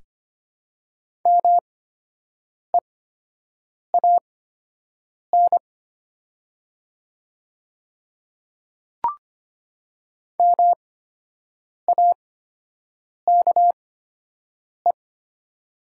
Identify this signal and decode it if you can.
Morse code, heard as a keyed tone, about 25 wpm, spaced as for 7 wpm, 700 Hz. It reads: MEAN MAKE
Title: 8x Character Spacing - Top 200 Words - 25wpm